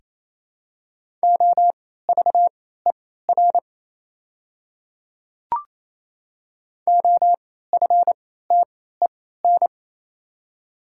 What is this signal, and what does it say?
Morse code, heard as a keyed tone, about 27 wpm, spaced as for 16 wpm, 700 Hz. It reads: OVER OFTEN
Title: 3x Character Spacing - Top 500 Words - 28wpm